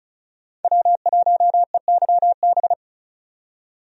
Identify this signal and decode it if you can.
Morse code, heard as a keyed tone, about 36 wpm, 700 Hz. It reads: W1EYB
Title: USA Call Signs - 5 Characters - 35wpm